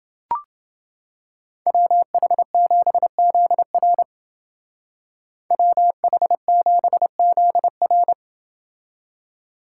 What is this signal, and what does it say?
Morse code, heard as a keyed tone, about 29 wpm, 700 Hz. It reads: WH7ZR WH7ZR